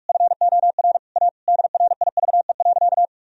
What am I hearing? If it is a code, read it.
Morse code, heard as a keyed tone, about 47 wpm, 700 Hz. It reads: FOR A DRIVE.
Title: Sentences from Top 1000 Words - Review Part 2 - 45wpm